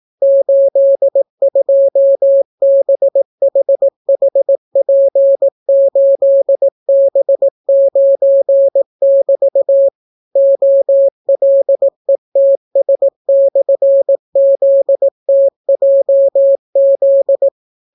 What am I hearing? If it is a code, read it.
Morse code, heard as a keyed tone, about 18 wpm, 550 Hz. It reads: 82BHHP8B9<BT> OLETS/ZTJZ